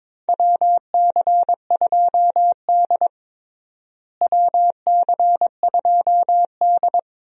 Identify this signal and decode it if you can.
Morse code, heard as a keyed tone, about 22 wpm, 700 Hz. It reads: WC2D WC2D
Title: USA Call Signs - 4 Characters - Repeated 3x - 22wpm